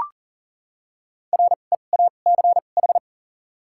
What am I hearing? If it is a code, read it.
Morse code, heard as a keyed tone, about 40 wpm, 700 Hz. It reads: R E A C H
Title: 2x Character Spacing - Top 500 Words - 40wpm